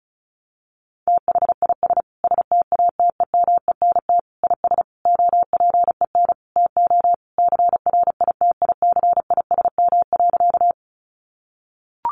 Morse code, heard as a keyed tone, about 34 wpm, 700 Hz. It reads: THIS STATEMENT IS OPEN TO CRITICISM.
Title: Sentences from Top 1300 Words 35wpm